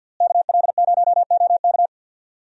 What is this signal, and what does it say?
Morse code, heard as a keyed tone, about 47 wpm, 700 Hz. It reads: KL0OK
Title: USA Call Signs - 5 Characters - Repeated 2x - 50wpm